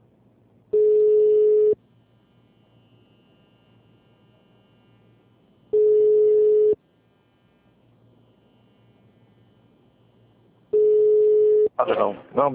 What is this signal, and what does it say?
Ringback tone (Europe, 425 Hz), with speech.